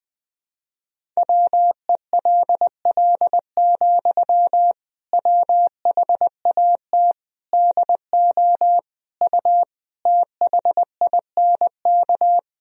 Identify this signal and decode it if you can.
Morse code, heard as a keyed tone, about 20 wpm, 700 Hz. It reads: WELL, WHAT DO U THINK
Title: Sentences from Top 200 Words and Abbreviations 20wpm